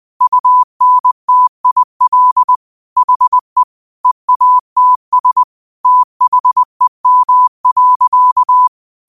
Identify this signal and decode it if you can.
Morse code, heard as a keyed tone, about 20 wpm, 1000 Hz. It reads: UNTIL HE EATS THEM.
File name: qotdfiles / 20wpm